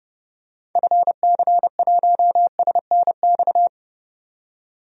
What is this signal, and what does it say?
Morse code, heard as a keyed tone, about 30 wpm, 700 Hz. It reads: FC1SNX
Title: DXCC-100 Call Signs - 6 Characters - 30wpm